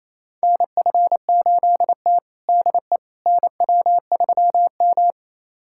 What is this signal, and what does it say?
Morse code, heard as a keyed tone, about 28 wpm, 700 Hz. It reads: NF8T DE NW3M
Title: USA Callsign DE Callsign - 4 Characters - 28wpm